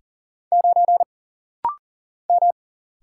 Morse code, heard as a keyed tone, about 40 wpm, 700 Hz. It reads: 9 M
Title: Single Letter-Number Alt - Rapid-Fire 40wpm